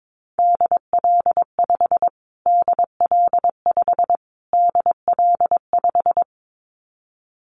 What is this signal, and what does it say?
Morse code, heard as a keyed tone, about 22 wpm, 700 Hz. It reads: DL5 DL5 DL5
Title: Callsign Prefixes for Top 25 DXCC - Repeated 3x - 22wpm